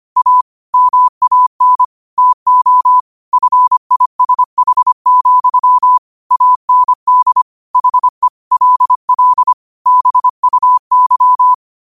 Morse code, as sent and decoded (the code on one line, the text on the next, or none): .- / -- .- -. / - --- / ..-. .. ... .... --..-- / .- -. -.. / .... / .-.. .-.. / -... ..- -.--
A MAN TO FISH, AND HE LL BUY